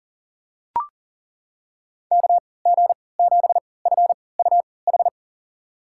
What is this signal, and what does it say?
Morse code, heard as a keyed tone, about 37 wpm, spaced as for 22 wpm, 700 Hz. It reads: KG7FUH
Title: USA Call Signs - 6 Characters - 3x Character Spacing - 40wpm